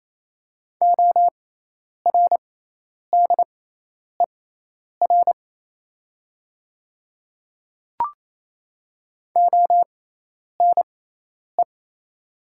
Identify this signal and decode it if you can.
Morse code, heard as a keyed tone, about 27 wpm, spaced as for 10 wpm, 690 Hz. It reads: ORDER ONE
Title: 6x Character Spacing - Top 500 Words - 28wpm